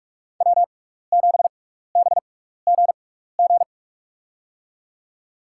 Morse code, heard as a keyed tone, about 44 wpm, spaced as for 15 wpm, 700 Hz. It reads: W7BGG
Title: USA Call Signs - 5 Characters - 6x Character Spacing - 45wpm